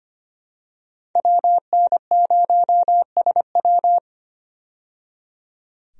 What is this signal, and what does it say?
Morse code, heard as a keyed tone, about 25 wpm, 700 Hz. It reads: WN0SW
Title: USA Call Signs - 5 Characters - 25wpm